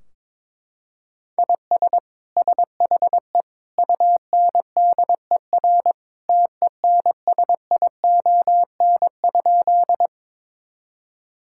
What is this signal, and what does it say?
Morse code, heard as a keyed tone, about 22 wpm, 700 Hz. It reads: IS SHE UNDER TENSION?
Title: Sentences from Top 1600 Words 22wpm